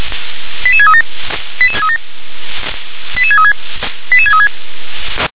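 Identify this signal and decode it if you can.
Five-tone selective call (ZVEI), heard as quick runs of five tones, tones of about 70 ms. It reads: 88637 88637 88636 88636
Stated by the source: Five-tone ZVEI selective call, received on SDR.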